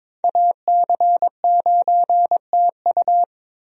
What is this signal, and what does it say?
Morse code, heard as a keyed tone, about 22 wpm, 700 Hz. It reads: AC9TU
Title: USA Call Signs - 5 Characters - Repeated 2x - 22wpm